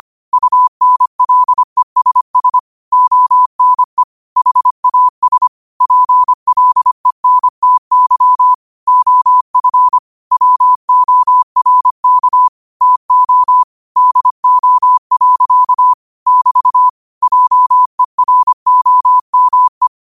Morse code, heard as a keyed tone, about 25 wpm, 1000 Hz. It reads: UNLESS ONE HAS PLENTY OF WORK TO DO. <BT> JEROME